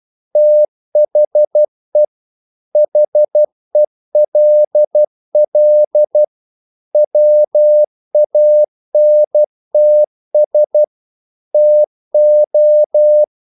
Morse code, heard as a keyed tone, about 12 wpm, 600 Hz. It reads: THE HELL WANTS TO
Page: English quotations in CW (telegraphy - morse)